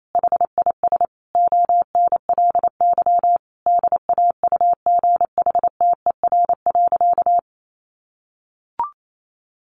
Morse code, heard as a keyed tone, about 28 wpm, 700 Hz. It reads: HIS ONLY DAUGHTER.
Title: Sentences from Top 1500 Words 28wpm